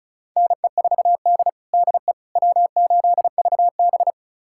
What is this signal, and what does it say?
Morse code, heard as a keyed tone, about 36 wpm, 700 Hz. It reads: NE4D DE W8VB
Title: USA Callsign DE Callsign - 4 Characters - 35wpm